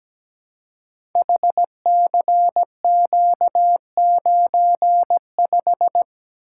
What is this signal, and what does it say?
Morse code, heard as a keyed tone, about 17 wpm, 700 Hz. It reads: HCQ95
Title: Code Groups - 5 Letter:Numbers - 17wpm